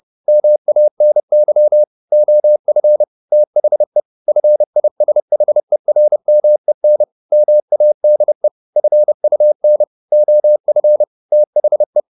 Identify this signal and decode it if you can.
Morse code, heard as a keyed tone, about 30 wpm, 600 Hz. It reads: MANY OF THE FISHERMEN MADE FUN OF THE
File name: the old man and the sea / the old man and the sea 1 30wpm